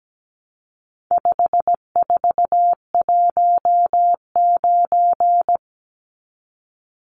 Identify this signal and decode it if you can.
Morse code, heard as a keyed tone, about 17 wpm, 700 Hz. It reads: 5419
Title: Numbers - 4-Digit 17wpm